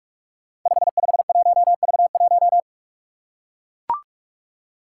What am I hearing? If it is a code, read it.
Morse code, heard as a keyed tone, about 44 wpm, 700 Hz. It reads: 551V1
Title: Five Challenging Letter-Number Combinations 45wpm